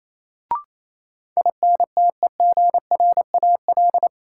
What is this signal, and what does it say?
Morse code, heard as a keyed tone, about 28 wpm, 700 Hz. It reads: INTEGRAL
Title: Sets of 5 Words - New Word Drill - 28wpm